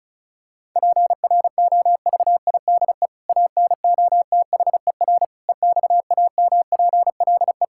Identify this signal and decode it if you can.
Morse code, heard as a keyed tone, about 36 wpm, 700 Hz. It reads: PROVIDE ANOTHER EXAMPLE
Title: Sentences from Top 1000 Words - Review Part 5 - 35wpm